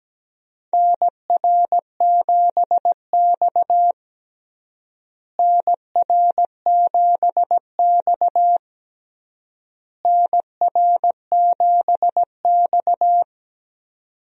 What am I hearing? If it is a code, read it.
Morse code, heard as a keyed tone, about 17 wpm, 700 Hz. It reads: NR7X NR7X NR7X
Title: USA Call Signs - 4 Characters - Repeated 3x - 17wpm